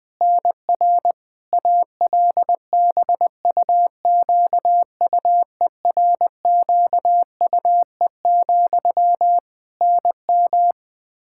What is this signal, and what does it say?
Morse code, heard as a keyed tone, about 20 wpm, 700 Hz. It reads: NR ALBUQUERQUE, NM
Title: Practice QSO - Exchange 1 - Beginning - 20wpm